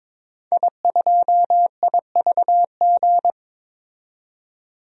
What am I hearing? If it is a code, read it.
Morse code, heard as a keyed tone, about 21 wpm, 700 Hz. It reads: I2IVG
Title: DXCC-100 Call Signs - 5 Characters - 22wpm